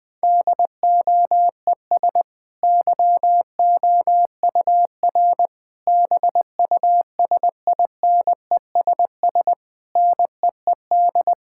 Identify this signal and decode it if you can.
Morse code, heard as a keyed tone, about 20 wpm, 700 Hz. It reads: DOES YOUR BUSINESS NEED
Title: Sentences from Top 1000 Words - Review Part 6 - 20wpm